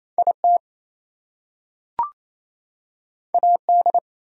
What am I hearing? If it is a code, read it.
Morse code, heard as a keyed tone, about 27 wpm, 700 Hz. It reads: IT AD
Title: Two Letter Words - 28wpm